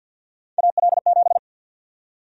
Morse code, heard as a keyed tone, about 51 wpm, 700 Hz. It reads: AL7
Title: USA Call Sign Prefixes - Repeated 2x - 50wpm